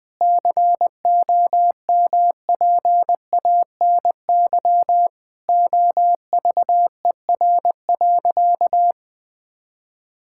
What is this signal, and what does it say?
Morse code, heard as a keyed tone, about 20 wpm, 700 Hz. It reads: COMPANY OVER.
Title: Sentences from Top 1000 Words - Review Part 6 - 20wpm